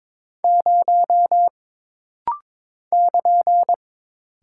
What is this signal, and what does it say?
Morse code, heard as a keyed tone, about 22 wpm, 700 Hz. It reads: 0 <KN>
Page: This rapid-fire timing is geared to push your ICR (Instant Character Recognition) skills.